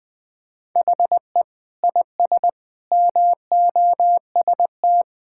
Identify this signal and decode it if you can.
Morse code, heard as a keyed tone, about 20 wpm, 700 Hz. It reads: HE IS MOST
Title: Sentences from Top 1600 Words 20wpm